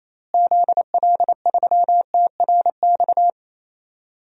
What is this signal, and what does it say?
Morse code, heard as a keyed tone, about 28 wpm, 700 Hz. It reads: ZL3TRX
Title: DXCC-100 Call Signs - 6 Characters - Repeated 3x - 28wpm